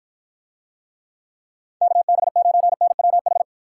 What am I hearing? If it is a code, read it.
Morse code, heard as a keyed tone, about 55 wpm, 700 Hz. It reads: KB9NWH